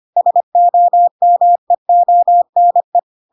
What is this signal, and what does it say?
Morse code, heard as a keyed tone, about 25 wpm, 700 Hz. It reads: SOMEONE
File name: Pinocchio / Pinocchio ch1 25 25 ik1hge